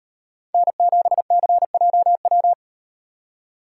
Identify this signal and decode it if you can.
Morse code, heard as a keyed tone, about 38 wpm, 700 Hz. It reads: N7CJW